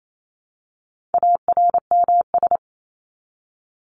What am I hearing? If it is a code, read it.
Morse code, heard as a keyed tone, about 28 wpm, 700 Hz. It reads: ARMS
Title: ES:S Suffix - Top 100 Words with Base Words 28wpm